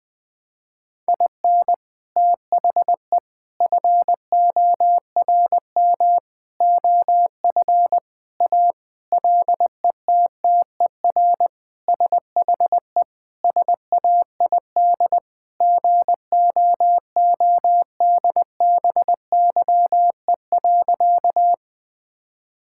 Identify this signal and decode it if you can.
Morse code, heard as a keyed tone, about 20 wpm, 700 Hz. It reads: IN THE FORM OF A LETTER SHE SAID GOODBYE.
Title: Sentences with 4-Word Phrases - Spoken Beginning - 20wpm